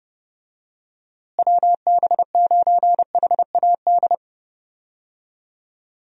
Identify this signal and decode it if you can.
Morse code, heard as a keyed tone, about 30 wpm, 700 Hz. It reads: WB9HAD
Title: USA Call Signs - 6 Characters - Repeated 3x - 30wpm